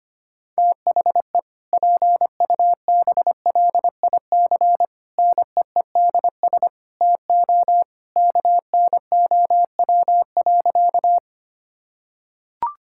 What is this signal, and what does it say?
Morse code, heard as a keyed tone, about 25 wpm, 700 Hz. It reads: THE PUBLIC NEEDS TO KNOW.
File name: practice / 25wpm / Sentences with 2-Word Phrases - 25wpm